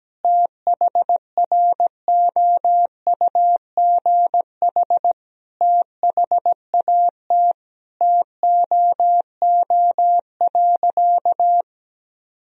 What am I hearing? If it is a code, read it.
Morse code, heard as a keyed tone, about 17 wpm, 700 Hz. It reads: THROUGH THAT TOO.
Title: Sentences from Top 1000 Words - Review Part 1 - 17wpm